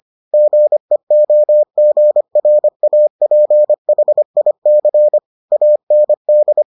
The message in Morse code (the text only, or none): GEOGRAPHIC AND